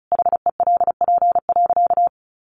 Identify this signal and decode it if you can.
Morse code, heard as a keyed tone, about 34 wpm, 700 Hz. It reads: HELP.
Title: Sentences from Top 1100 Words 35wpm